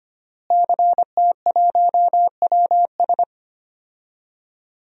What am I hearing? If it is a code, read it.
Morse code, heard as a keyed tone, about 25 wpm, 700 Hz. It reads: CT1WS
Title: DXCC-100 Call Signs - 5 Characters - Repeated 2x - 25wpm